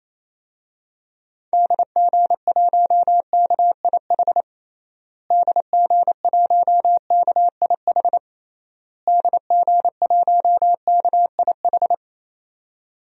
Morse code, heard as a keyed tone, about 28 wpm, 700 Hz. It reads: DG1KIH DG1KIH DG1KIH
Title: DXCC-100 Call Signs - 6 Characters - Repeated 3x - 28wpm